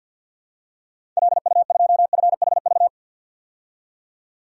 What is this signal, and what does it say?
Morse code, heard as a keyed tone, about 53 wpm, 700 Hz. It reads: LU2FHV
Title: DXCC-100 Call Signs - 6 Characters - 50wpm